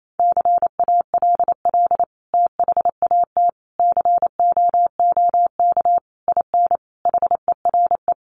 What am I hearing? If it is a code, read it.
Morse code, heard as a keyed tone, about 28 wpm, 700 Hz. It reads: CALL THAT COOK IN HERE
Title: Sentences from Top 1100 Words 28wpm